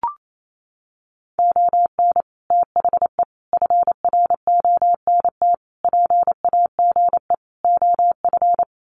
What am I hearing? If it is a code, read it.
Morse code, heard as a keyed tone, about 28 wpm, 690 Hz. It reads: ON THE FRONT PAGE OF
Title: Sets of 5 Words Encore 28wpm